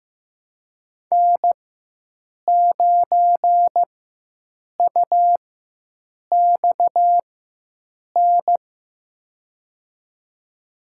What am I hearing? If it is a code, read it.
Morse code, heard as a keyed tone, about 15 wpm, spaced as for 7 wpm, 700 Hz. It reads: N9UXN
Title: USA Call Signs - 5 Characters - 4x Character Spacing - 15wpm